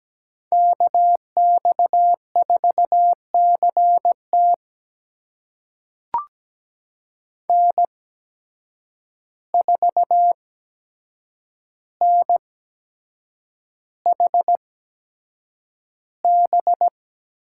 Morse code, heard as a keyed tone, about 17 wpm, spaced as for 7 wpm, 700 Hz. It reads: KX4CT N4NHB
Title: USA Call Signs - 5 Characters - 8x Character Spacing - 17wpm